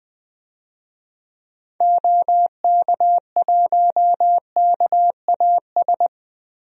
Morse code, heard as a keyed tone, about 20 wpm, 700 Hz. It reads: OK1KAS
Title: DXCC-100 Call Signs - 6 Characters - Repeated 3x - 20wpm